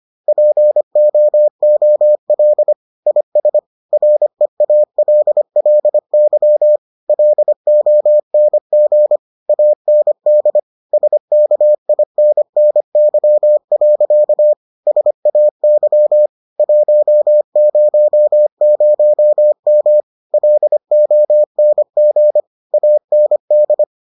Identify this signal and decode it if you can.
Morse code, heard as a keyed tone, about 25 wpm, 600 Hz. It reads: POOL IS REALLY LONG AND SKINNY. SAY 100M LONG AND